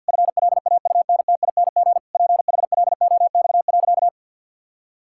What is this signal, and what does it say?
Morse code, heard as a keyed tone, about 48 wpm, 700 Hz. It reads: FLAUNTING PHLOX.
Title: Pangram Challenge v2 - 50wpm